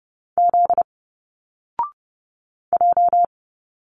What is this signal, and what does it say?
Morse code, heard as a keyed tone, about 30 wpm, 700 Hz. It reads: Z J
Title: Single Letters 30wpm